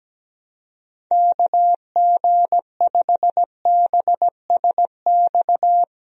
Morse code, heard as a keyed tone, about 17 wpm, 700 Hz. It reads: KG5BSX
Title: USA Call Signs - 6 Characters - Repeated 3x - 17wpm